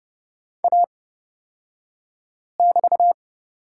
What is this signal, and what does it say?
Morse code, heard as a keyed tone, about 29 wpm, spaced as for 5 wpm, 700 Hz. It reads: A<BT>
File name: Lesson 36d - BT - Two Characters 30wpm